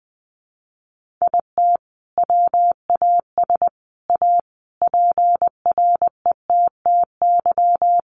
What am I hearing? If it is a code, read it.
Morse code, heard as a keyed tone, about 20 wpm, 700 Hz. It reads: IT WAS A PRETTY
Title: Sentences from Top 1100 Words 20wpm